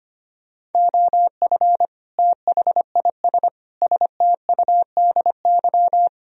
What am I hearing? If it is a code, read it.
Morse code, heard as a keyed tone, about 30 wpm, 700 Hz. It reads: OF THIS STUDY
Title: Speed-Racing - Sets of 5 Words 25wpm